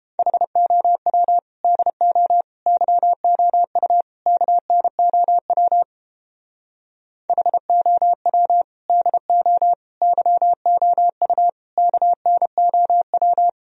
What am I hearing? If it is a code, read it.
Morse code, heard as a keyed tone, about 29 wpm, 700 Hz. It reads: HOW DO YOU KNOW HOW DO YOU KNOW